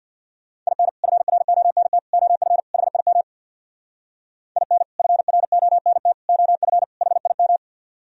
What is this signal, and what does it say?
Morse code, heard as a keyed tone, about 64 wpm, 700 Hz. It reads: IN FRONT OF HIM IN FRONT OF HIM